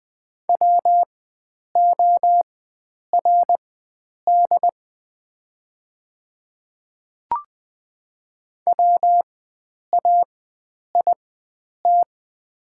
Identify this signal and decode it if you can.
Morse code, heard as a keyed tone, about 20 wpm, spaced as for 9 wpm, 700 Hz. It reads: WORD WAIT
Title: 4x Character Spacing - Top 500 Words - 20wpm